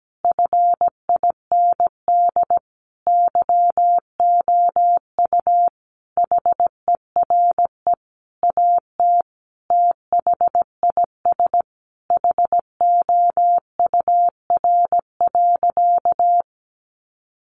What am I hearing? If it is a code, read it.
Morse code, heard as a keyed tone, about 17 wpm, 700 Hz. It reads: FIND YOU HERE AT THIS HOUR.
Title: Sentences from Top 1400 Words 17wpm